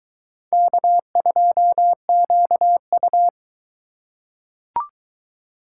Morse code, heard as a keyed tone, about 22 wpm, 700 Hz. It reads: K2QU